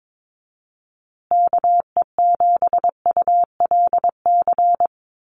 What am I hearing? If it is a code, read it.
Morse code, heard as a keyed tone, about 22 wpm, 700 Hz. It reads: KE7ULC
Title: USA Call Signs Repeated 2x 22wpm